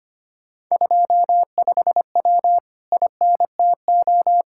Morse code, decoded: 25W INTO

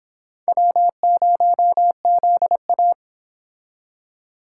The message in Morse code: W0ZA

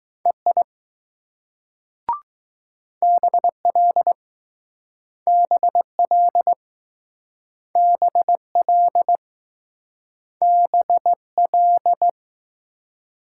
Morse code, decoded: EI BL BL BL BL

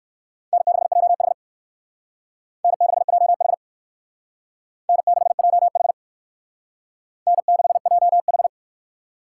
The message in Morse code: N6JH N6JH N6JH N6JH

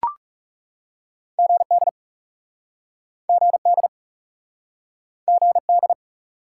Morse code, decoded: GD GD GD

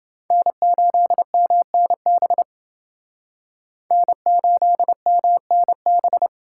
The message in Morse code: N8MNB N8MNB

